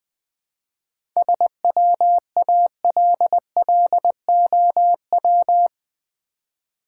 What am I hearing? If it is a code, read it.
Morse code, heard as a keyed tone, about 20 wpm, 700 Hz. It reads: SWALLOW
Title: 7 Letter Words - First 3 Letters Given - 20wpm